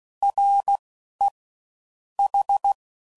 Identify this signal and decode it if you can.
Morse code, heard as a keyed tone, about 16 wpm, 800 Hz. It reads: R E H